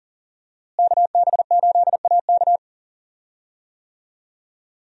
Morse code, heard as a keyed tone, about 39 wpm, 700 Hz. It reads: KB8AK
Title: USA Call Signs - 5 Characters - Repeated 3x - 40wpm